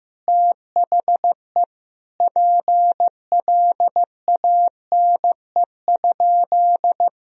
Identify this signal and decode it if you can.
Morse code, heard as a keyed tone, about 15 wpm, 700 Hz. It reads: THE PLANE?
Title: Sentences from Top 1000 Words - Review Part 3 - 15wpm